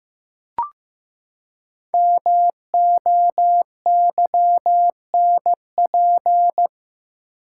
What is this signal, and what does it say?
Morse code, heard as a keyed tone, about 15 wpm, 700 Hz. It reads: MOYNP
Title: Code Groups - 5 Letter:Numbers - 15wpm